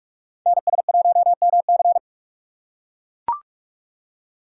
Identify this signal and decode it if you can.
Morse code, heard as a keyed tone, about 43 wpm, 700 Hz. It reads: NS1MC
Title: USA Call Signs - 5 Characters - 6x Character Spacing - 45wpm